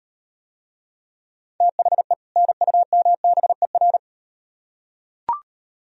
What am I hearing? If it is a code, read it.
Morse code, heard as a keyed tone, about 38 wpm, 700 Hz. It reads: THE NUMBER